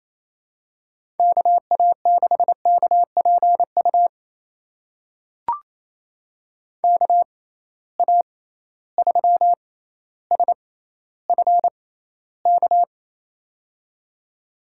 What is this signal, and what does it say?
Morse code, heard as a keyed tone, about 28 wpm, spaced as for 14 wpm, 700 Hz. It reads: KA6KPU KA3SFK
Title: USA Call Signs - 6 Characters - 6x Character Spacing - 28wpm